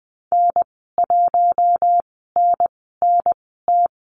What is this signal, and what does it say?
Morse code, heard as a keyed tone, about 20 wpm, spaced as for 15 wpm, 700 Hz. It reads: N1NNT
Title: 186 Lesson 30g - Letter b - Call Signs 20wpm